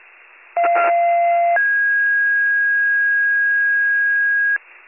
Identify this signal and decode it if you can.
Two-tone page, heard as a one-second tone, then a three-second tone, about 690 then 1650 Hz, with noise. It is heard from 0.6 to 4.6 s.